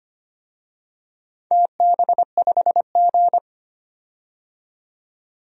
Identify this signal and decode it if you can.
Morse code, heard as a keyed tone, about 25 wpm, 700 Hz. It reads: TB5G
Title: DXCC-100 Call Signs - 4 Characters - 25wpm